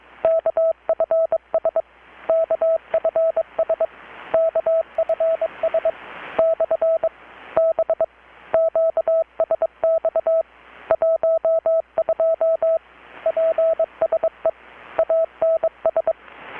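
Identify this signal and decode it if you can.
Morse code, heard as a keyed tone, about 22 wpm, 650 Hz, with radio noise. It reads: KFS KFS KFS / B QSX 12 PSE ANS
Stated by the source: Morse from KPH's Night of Nights XX maritime radiotelegraphy event, received in Chiapas, Mexico about 3,800 km away on a Kenwood R-600 with a grounded 10 m long-wire antenna.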